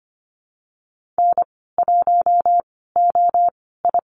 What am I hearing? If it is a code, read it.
Morse code, heard as a keyed tone, about 25 wpm, spaced as for 16 wpm, 700 Hz. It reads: N1OI
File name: Lesson 15g - Number 5 - Call Signs 25wpm